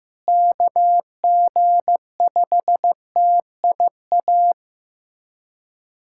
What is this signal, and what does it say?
Morse code, heard as a keyed tone, about 15 wpm, 700 Hz. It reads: KG5TIA
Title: USA Callsign DE Callsign 15wpm